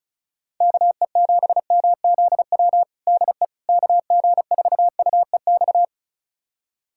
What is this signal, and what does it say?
Morse code, heard as a keyed tone, about 34 wpm, 700 Hz. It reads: KE7MZW DE KG4UEX
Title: USA Callsign DE Callsign - 6 Characters - 35wpm